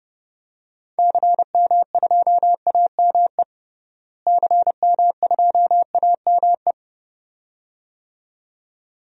Morse code, heard as a keyed tone, about 31 wpm, 700 Hz. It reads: CM2AME CM2AME
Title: DXCC-100 Call Signs - 6 Characters - Repeated 2x - 30wpm